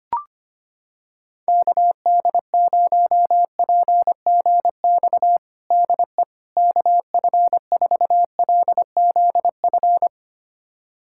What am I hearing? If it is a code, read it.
Morse code, heard as a keyed tone, about 25 wpm, 700 Hz. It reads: KD0PGX DE KF4LZF
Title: USA Callsign DE Callsign 25wpm